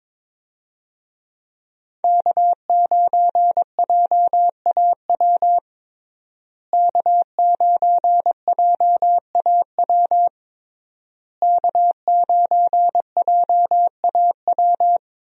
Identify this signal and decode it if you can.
Morse code, heard as a keyed tone, about 21 wpm, 700 Hz. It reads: K9JAW K9JAW K9JAW
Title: USA Call Signs - 5 Characters - Repeated 3x - 22wpm